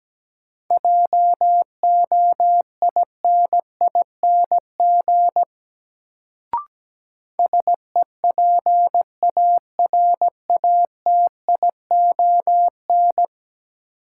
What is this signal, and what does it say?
Morse code, heard as a keyed tone, about 17 wpm, 700 Hz. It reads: JOINING SEPARATION
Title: Sets of 5 Words - New Word Drill - 17wpm